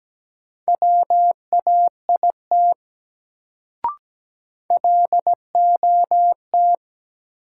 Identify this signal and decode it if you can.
Morse code, heard as a keyed tone, about 17 wpm, 700 Hz. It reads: WAIT LOT
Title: Sentences from Top 500 Words - New Word Drill - 17wpm